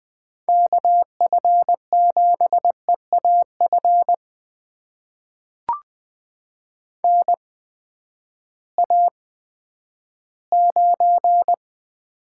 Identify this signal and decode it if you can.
Morse code, heard as a keyed tone, about 20 wpm, spaced as for 11 wpm, 700 Hz. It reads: KF7EAF NA9
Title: USA Call Signs - 8x Character Spacing - 20wpm